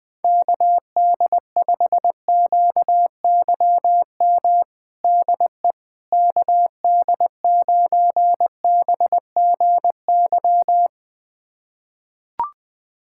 Morse code, heard as a keyed tone, about 20 wpm, 700 Hz. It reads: KD5QYM DE KD9BGY
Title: USA Callsign DE Callsign - 6 Characters - 20wpm